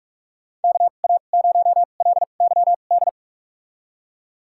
Morse code, heard as a keyed tone, about 44 wpm, spaced as for 32 wpm, 700 Hz. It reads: KA0LYD
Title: USA Call Signs - 2x Character Spacing - 45wpm